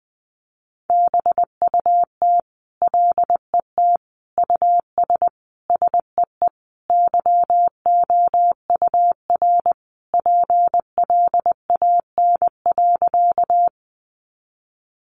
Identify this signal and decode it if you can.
Morse code, heard as a keyed tone, about 20 wpm, 700 Hz. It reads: BUT LET US SEE YOUR PLAN.
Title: Sentences from Top 500 Words 20wpm